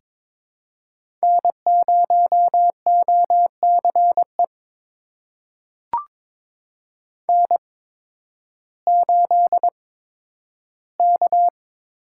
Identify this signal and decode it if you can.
Morse code, heard as a keyed tone, about 23 wpm, spaced as for 12 wpm, 700 Hz. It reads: N0OCE N8K
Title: USA Call Signs - 8x Character Spacing - 22wpm